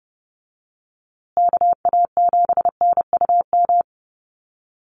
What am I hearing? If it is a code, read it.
Morse code, heard as a keyed tone, about 30 wpm, 700 Hz. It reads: KA7NUM
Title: USA Call Signs Repeated 2x 30wpm